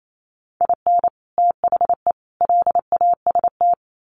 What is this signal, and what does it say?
Morse code, heard as a keyed tone, about 28 wpm, 700 Hz. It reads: IN THE LAST
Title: Sets of 5 Words Encore2 28wpm